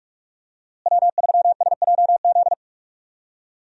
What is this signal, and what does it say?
Morse code, heard as a keyed tone, about 46 wpm, 700 Hz. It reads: W3SJZ
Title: USA Call Signs - 5 Characters - Repeated 2x - 45wpm